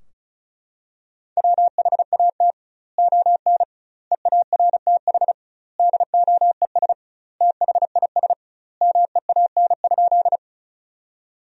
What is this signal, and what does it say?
Morse code, heard as a keyed tone, about 36 wpm, 700 Hz. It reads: WHAT ON EARTH DOES THIS MEAN?
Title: 2x Word-Spacing - Sentences from Top 200 Words - 35wpm